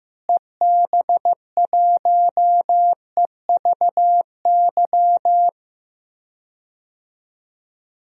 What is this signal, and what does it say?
Morse code, heard as a keyed tone, about 15 wpm, 700 Hz. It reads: EB1EVY
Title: DXCC-100 Call Signs - 6 Characters - Repeated 3x - 15wpm